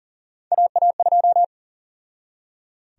Morse code, heard as a keyed tone, about 39 wpm, 700 Hz. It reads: AA2
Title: USA Call Sign Prefixes - 40wpm